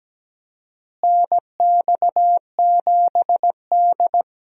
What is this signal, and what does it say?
Morse code, heard as a keyed tone, about 17 wpm, 700 Hz. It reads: NX7D
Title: USA Call Signs - 4 Characters - Repeated 2x - 17wpm